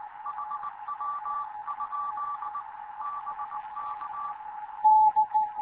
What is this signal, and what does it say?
Morse code, heard as a keyed tone, about 19 wpm, 1160 Hz. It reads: HW? BK
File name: Morze-HA2MN / hangp / 2 11 GYAKORLAT